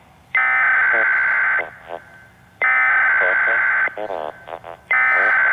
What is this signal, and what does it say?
Emergency Alert System (SAME) tones (header bursts), heard as harsh buzzing bursts, with speech. It is heard from 0.4 to 5.3 s.